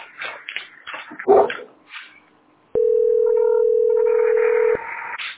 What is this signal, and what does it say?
Ringback tone (North America), 440 + 480 Hz.